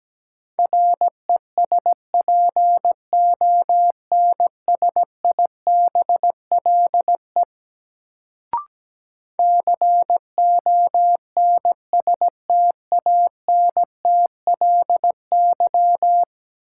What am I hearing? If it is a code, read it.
Morse code, heard as a keyed tone, about 17 wpm, 700 Hz. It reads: RESPONSIBLE CONSTANTLY
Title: Sentences from Top 1000 Words - Review Part 7 - New Word Drill - 17wpm